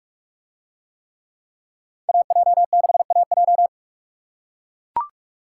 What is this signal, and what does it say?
Morse code, heard as a keyed tone, about 41 wpm, 700 Hz. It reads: AJ6AJ